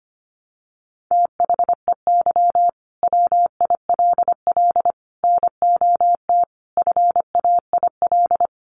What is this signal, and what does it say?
Morse code, heard as a keyed tone, about 25 wpm, 690 Hz. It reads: THEY WILL NOT FAIL
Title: Sentences from Top 1100 Words 25wpm